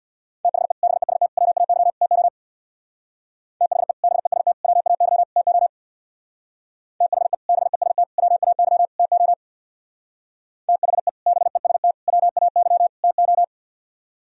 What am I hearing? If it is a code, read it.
Morse code, heard as a keyed tone, about 58 wpm, 700 Hz. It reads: THE BEST WAY TO THE BEST WAY TO THE BEST WAY TO THE BEST WAY TO
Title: Speed-Racing - Sets of 4 Words 50wpm